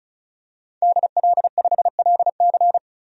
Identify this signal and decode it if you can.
Morse code, heard as a keyed tone, about 36 wpm, 700 Hz. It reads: DL5LC
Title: DXCC-100 Call Signs - 5 Characters - 35wpm